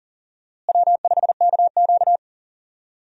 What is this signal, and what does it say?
Morse code, heard as a keyed tone, about 40 wpm, 700 Hz. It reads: W5KQ
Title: USA Call Signs - 4 Characters - Repeated 3x - 40wpm